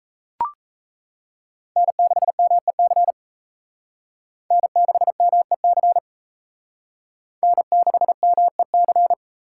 Morse code, heard as a keyed tone, about 35 wpm, 700 Hz. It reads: N6MEC N6MEC N6MEC